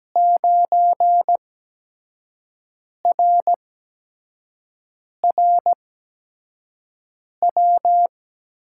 Morse code, heard as a keyed tone, about 17 wpm, spaced as for 5 wpm, 700 Hz. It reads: 9RRW
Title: USA Call Signs - 8x Character Spacing - 17wpm